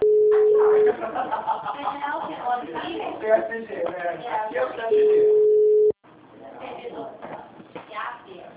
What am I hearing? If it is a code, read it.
Ringback tone (Europe, 425 Hz), with speech.